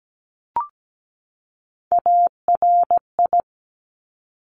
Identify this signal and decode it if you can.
Morse code, heard as a keyed tone, about 17 wpm, 700 Hz. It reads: ARI
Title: Top Three Letter Combinations in Words 17wpm